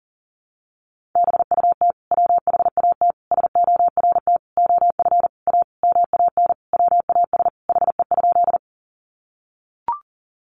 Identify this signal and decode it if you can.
Morse code, heard as a keyed tone, about 40 wpm, 700 Hz. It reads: BUT WHAT SORT OF A MAN WAS HE?
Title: Sentences from Top 1300 Words 40wpm